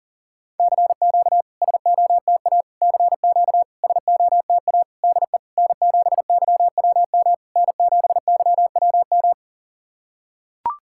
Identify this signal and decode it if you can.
Morse code, heard as a keyed tone, about 40 wpm, 700 Hz. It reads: CQ SOTA CQ SOTA DE N7YWM N7YWM
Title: SOTA QSO Practice 40wpm